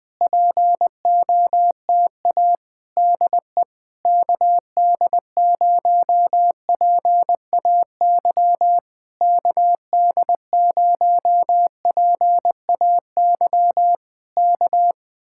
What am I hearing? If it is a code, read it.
Morse code, heard as a keyed tone, about 20 wpm, 700 Hz. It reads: POTA DE KD0PAY KD0PAY K
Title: Speed-Racing Common QSO Phrases - 17wpm